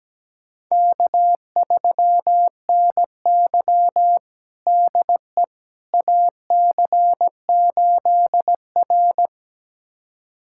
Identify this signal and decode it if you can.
Morse code, heard as a keyed tone, about 17 wpm, 700 Hz. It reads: K3NY DE AC8R
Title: USA Callsign DE Callsign - 4 Characters - 17wpm